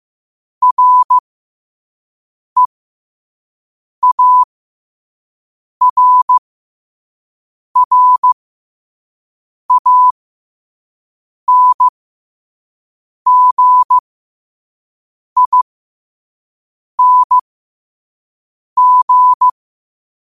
Morse code, heard as a keyed tone, about 15 wpm, spaced as for 5 wpm, 1000 Hz. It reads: REARRANGING